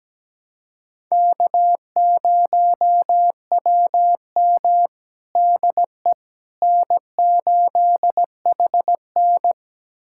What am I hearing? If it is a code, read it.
Morse code, heard as a keyed tone, about 17 wpm, 700 Hz. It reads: K0WM DE N8HN
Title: USA Callsign DE Callsign - 4 Characters - 17wpm